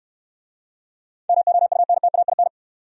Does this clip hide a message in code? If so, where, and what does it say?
Yes: Morse code from 1.3 to 2.5 s, heard as a keyed tone, about 67 wpm, 700 Hz. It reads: MOUNTAIN